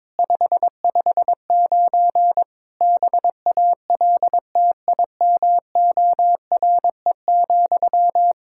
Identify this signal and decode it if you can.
Morse code, heard as a keyed tone, about 22 wpm, 700 Hz. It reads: 559 BALTIMORE,
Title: Practice QSO - Exchange 1 - Beginning - 22wpm